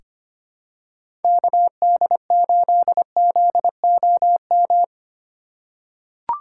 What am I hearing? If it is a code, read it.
Morse code, heard as a keyed tone, about 25 wpm, 700 Hz. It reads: KD8ZOM
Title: USA Call Signs - 6 Characters - 2x Character Spacing - 25wpm